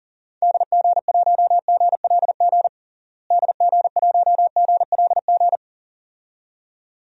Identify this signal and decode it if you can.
Morse code, heard as a keyed tone, about 40 wpm, 700 Hz. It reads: DG1GLG DG1GLG
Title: DXCC-100 Call Signs - 6 Characters - Repeated 2x - 40wpm